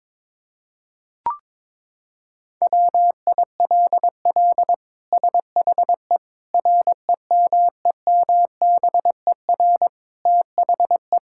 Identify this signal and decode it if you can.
Morse code, heard as a keyed tone, about 22 wpm, 700 Hz. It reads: WILL SHE REMEMBER THE
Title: Sentences from Top 1000 Words - Review Part 6 - 22wpm